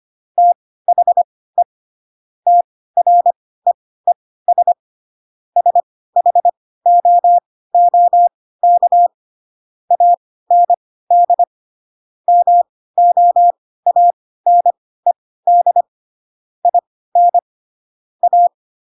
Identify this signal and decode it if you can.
Morse code, heard as a keyed tone, about 25 wpm, spaced as for 16 wpm, 700 Hz. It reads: THE TREES SHOOK AND MOANED IN A
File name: Pinocchio / Pinocchio ch6 10 25 ik1hge